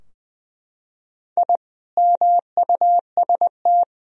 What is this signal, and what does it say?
Morse code, heard as a keyed tone, about 20 wpm, 700 Hz. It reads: I MUST
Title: Sentences from Top 1000 Words - Review Part 3 - 20wpm